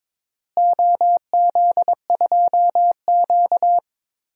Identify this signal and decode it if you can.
Morse code, heard as a keyed tone, about 22 wpm, 700 Hz. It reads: OZ2Q
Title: DXCC-100 Call Signs - 4 Characters - Repeated 3x - 22wpm